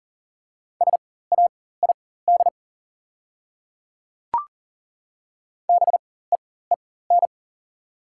Morse code, heard as a keyed tone, about 37 wpm, spaced as for 18 wpm, 690 Hz. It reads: SAID BEEN
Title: 4x Character Spacing - Top 100 Words - 40wpm